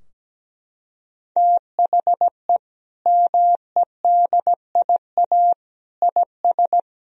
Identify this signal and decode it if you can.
Morse code, heard as a keyed tone, about 17 wpm, 700 Hz. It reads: THE MEDIA IS
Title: Sentences from Top 1000 Words - Review Part 4 - 17wpm